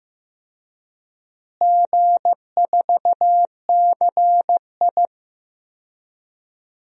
Morse code, heard as a keyed tone, about 15 wpm, 700 Hz. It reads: G4CI
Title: DXCC-100 Call Signs - 4 Characters - Repeated 2x - 15wpm